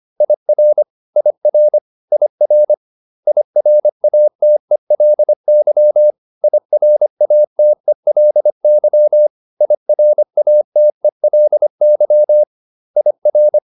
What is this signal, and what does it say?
Morse code, heard as a keyed tone, about 25 wpm, 600 Hz. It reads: IR IR IR IRATELY IRATELY IRATELY IR